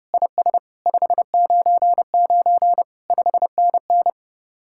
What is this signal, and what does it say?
Morse code, heard as a keyed tone, about 35 wpm, 700 Hz. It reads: IS 599 5NN
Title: Speed-Racing Common QSO Phrases - 30wpm